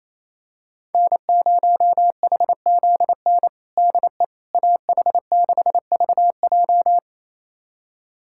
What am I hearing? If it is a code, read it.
Morse code, heard as a keyed tone, about 27 wpm, 700 Hz. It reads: N0HZN DE AH6VJ
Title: USA Callsign DE Callsign - 5 Characters - 28wpm